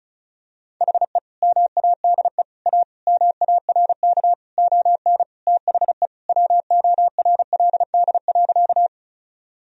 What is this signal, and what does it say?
Morse code, heard as a keyed tone, about 36 wpm, 700 Hz. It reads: HE MADE A MARK ON THE WORLD.
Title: Sentences from Top 1000 Words - Review Part 1 - 35wpm